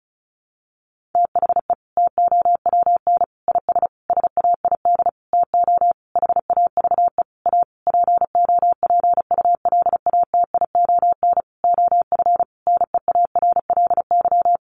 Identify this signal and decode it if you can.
Morse code, heard as a keyed tone, about 34 wpm, 700 Hz. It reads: THE TOWN IS SAID TO HAVE A POPULATION OF NEARLY
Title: Sentences from Top 1300 Words 35wpm